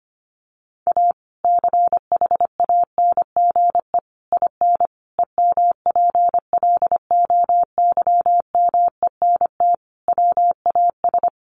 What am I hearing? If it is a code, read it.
Morse code, heard as a keyed tone, about 25 wpm, 700 Hz. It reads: A CHANGE IN EMPLOYMENT WAS
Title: Sentences from Top 1500 Words 25wpm